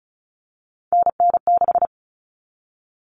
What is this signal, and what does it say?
Morse code, heard as a keyed tone, about 34 wpm, 700 Hz. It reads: NN6
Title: All Cut Numbers - 3-Digit 35wpm